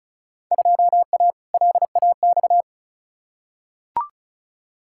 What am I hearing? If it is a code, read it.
Morse code, heard as a keyed tone, about 36 wpm, 700 Hz. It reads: 2A LAX
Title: Field Day Exchanges - 35wpm